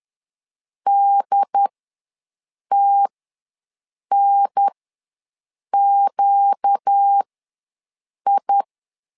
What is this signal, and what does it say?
Morse code, heard as a keyed tone, about 10 wpm, spaced as for 6 wpm, 800 Hz. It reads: DTNQI
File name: LEZIONI / 4 Q ws 10wpm 2306